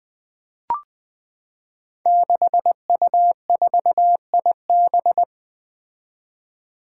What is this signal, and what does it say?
Morse code, heard as a keyed tone, about 20 wpm, 700 Hz. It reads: 6U4IB
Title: Five Challenging Letter-Number Combinations 20wpm